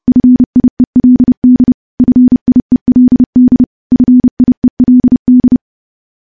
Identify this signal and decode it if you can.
Morse code, heard as a keyed tone, about 30 wpm, 250 Hz. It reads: FIELD FIELD FIELD
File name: morse / morse-training-30WPM